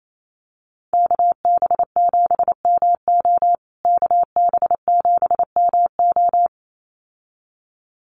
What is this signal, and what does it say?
Morse code, heard as a keyed tone, about 28 wpm, 700 Hz. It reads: KB7MO KB7MO
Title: USA Call Signs Repeated 2x 28wpm